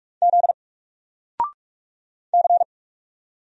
Morse code, heard as a keyed tone, about 45 wpm, 700 Hz. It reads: Z C
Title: Single Letter-Number 45wpm